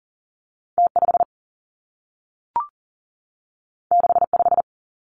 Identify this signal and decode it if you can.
Morse code, heard as a keyed tone, about 42 wpm, 700 Hz. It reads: T 5 6 5
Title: Two Challenging Letter-Number Combinations 40wpm